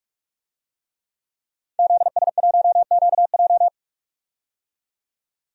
Morse code, heard as a keyed tone, about 47 wpm, 700 Hz. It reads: ZS1QJ
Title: DXCC-100 Call Signs - 5 Characters - 45wpm